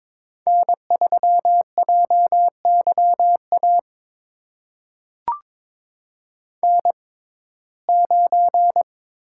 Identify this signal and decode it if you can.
Morse code, heard as a keyed tone, about 22 wpm, spaced as for 16 wpm, 700 Hz. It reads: N3JYA N9
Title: USA Call Signs - 5 Characters - 6x Character Spacing - 22wpm